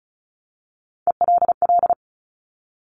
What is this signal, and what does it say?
Morse code, heard as a keyed tone, about 36 wpm, 700 Hz. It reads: ELL
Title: Top Three Letter Combinations in Words 35wpm